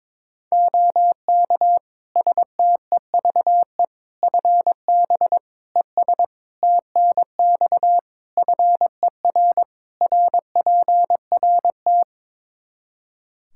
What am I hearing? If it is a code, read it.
Morse code, heard as a keyed tone, about 21 wpm, 700 Hz. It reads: OK STEVE FB ES TNX FER RPRT
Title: Practice QSO - Exchange 2 - Info - 22wpm